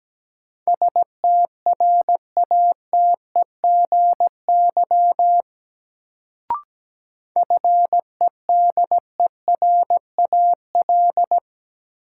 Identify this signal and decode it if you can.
Morse code, heard as a keyed tone, about 17 wpm, 700 Hz. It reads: STRATEGY FEDERAL
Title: Sentences from Top 1000 Words - Review Part 5 - New Word Drill - 17wpm